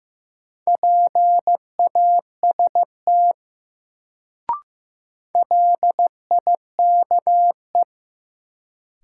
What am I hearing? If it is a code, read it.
Morse code, heard as a keyed tone, about 15 wpm, 700 Hz. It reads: PAST LIKE
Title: Sets of 2 Words - New Word Drill - 15wpm